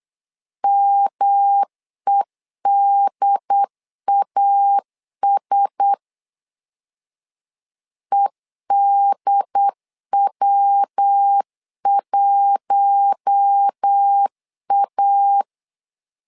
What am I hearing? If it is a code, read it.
Morse code, heard as a keyed tone, about 8 wpm, 800 Hz. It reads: MEDAS EDW1A
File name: LEZIONI / 4 Q1S 8wpm 2710